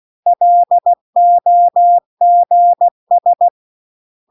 Morse code, heard as a keyed tone, about 16 wpm, 700 Hz. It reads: LOGS